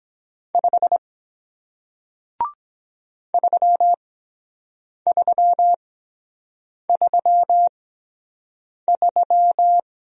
Morse code, heard as a keyed tone, about 23 wpm, spaced as for 7 wpm, 700 Hz. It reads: 5 3333